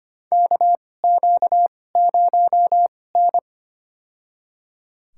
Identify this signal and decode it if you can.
Morse code, heard as a keyed tone, about 25 wpm, spaced as for 18 wpm, 700 Hz. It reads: KQ0N
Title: USA Call Signs - 4 Characters - 2x Character Spacing - 25wpm